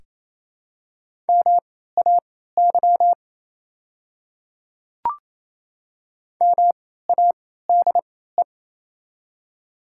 Morse code, heard as a keyed tone, about 27 wpm, spaced as for 16 wpm, 700 Hz. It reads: MAY MADE
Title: 3x Character Spacing - Top 300 Words - 28wpm